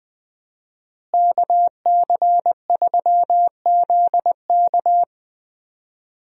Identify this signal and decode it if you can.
Morse code, heard as a keyed tone, about 20 wpm, 700 Hz. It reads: KC3ZK
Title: USA Call Signs - 5 Characters - Repeated 2x - 20wpm